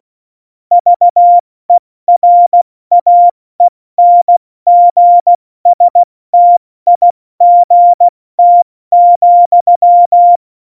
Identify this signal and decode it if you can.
Morse code, heard as a keyed tone, about 16 wpm, spaced as for 14 wpm, 700 Hz. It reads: VERAENGSTIGT,
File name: Morsekurs / Witze / Witz 22. Dezember 2020, 70 F80